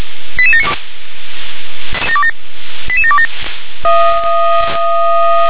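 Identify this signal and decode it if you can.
Five-tone selective call (ZVEI), heard as quick runs of five tones, tones of about 70 ms. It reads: 88792 88727
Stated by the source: ZVEI five-tone selective call 88768 received by SDR.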